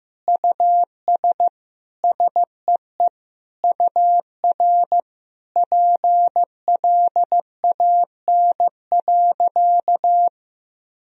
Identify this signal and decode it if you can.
Morse code, heard as a keyed tone, about 15 wpm, 700 Hz. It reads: US SEE UR PLAN.
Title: Sentences from Top 500 Words and Abbreviations 15wpm